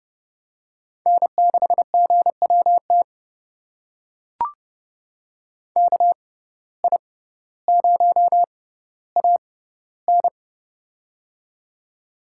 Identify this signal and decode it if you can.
Morse code, heard as a keyed tone, about 30 wpm, spaced as for 15 wpm, 700 Hz. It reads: N6GWT KI0AN
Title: USA Call Signs - 5 Characters - 6x Character Spacing - 30wpm